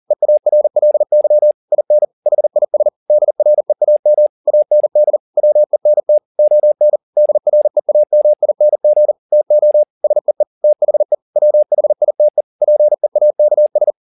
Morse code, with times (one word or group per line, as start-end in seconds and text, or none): 0.1-1.5 EARLY
1.7-2.0 IN
2.3-2.9 HIS
3.1-4.3 DREAM
4.5-5.2 AND
5.4-6.2 WENT
6.4-7.0 ON
7.2-9.1 DREAMING
9.3-9.8 TO
10.0-10.4 SEE
10.6-11.1 THE
11.4-12.4 WHITE
12.6-13.9 PEAKS